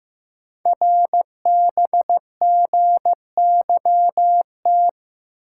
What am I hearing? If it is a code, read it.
Morse code, heard as a keyed tone, about 15 wpm, 700 Hz. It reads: RBGYT
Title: Code Groups - 5 Letters - 15wpm